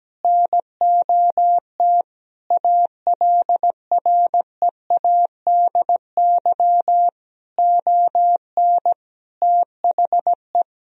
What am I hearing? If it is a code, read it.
Morse code, heard as a keyed tone, about 17 wpm, 700 Hz. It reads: NOT ALREADY ON THE
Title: Sentences from Top 1000 Words - Review Part 1 - 17wpm